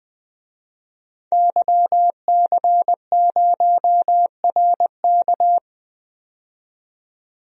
Morse code, heard as a keyed tone, about 20 wpm, 700 Hz. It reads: YC0RK
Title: DXCC-100 Call Signs - 5 Characters - 20wpm